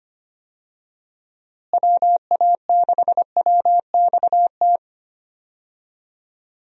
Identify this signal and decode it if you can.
Morse code, heard as a keyed tone, about 25 wpm, 700 Hz. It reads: WA6WXT
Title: USA Call Signs - 6 Characters - Repeated 2x - 25wpm